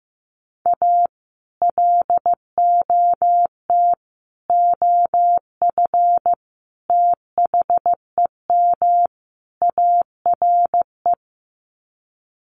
Morse code, heard as a keyed tone, about 15 wpm, 700 Hz. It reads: A LOT OF THEM ARE
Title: Sets of 5 Words Encore2 15wpm